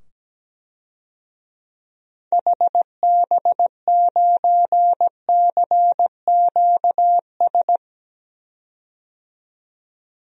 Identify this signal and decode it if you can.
Morse code, heard as a keyed tone, about 17 wpm, 700 Hz. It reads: HB9CQS
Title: DXCC-100 Call Signs - 6 Characters - 17wpm